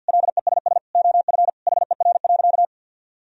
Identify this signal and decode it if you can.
Morse code, heard as a keyed tone, about 50 wpm, 700 Hz. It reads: LESS OF HER.
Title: Sentences from Top 1000 Words - Review Part 2 - 50wpm